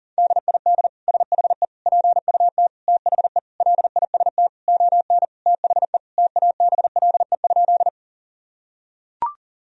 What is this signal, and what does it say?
Morse code, heard as a keyed tone, about 40 wpm, 700 Hz. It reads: DID SHE PUT THE LIST ON THE TABLE?
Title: Sentences from Top 1000 Words - Review Part 2 - 40wpm